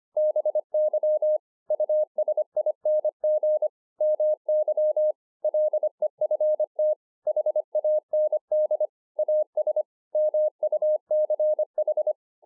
Morse code, heard as a keyed tone, about 25 wpm, 610 Hz. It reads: BY USING MY LEFT HAND AS MUCH